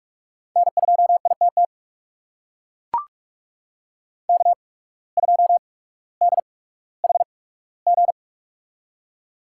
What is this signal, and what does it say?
Morse code, heard as a keyed tone, about 44 wpm, spaced as for 19 wpm, 700 Hz. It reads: N2ITT K2DHG
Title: USA Call Signs - 8x Character Spacing - 45wpm